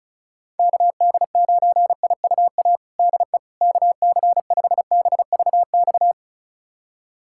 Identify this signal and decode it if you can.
Morse code, heard as a keyed tone, about 36 wpm, 700 Hz. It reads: KD9IUA DE KC5BVX
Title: USA Callsign DE Callsign 35wpm